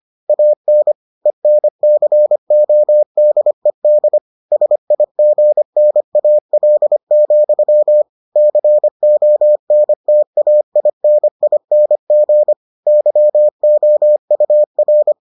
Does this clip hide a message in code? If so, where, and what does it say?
Yes: Morse code from 0.3 to 15.1 s, heard as a keyed tone, about 25 wpm, 600 Hz. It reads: AN ENCODED SIGNAL, CONTAINING YOUR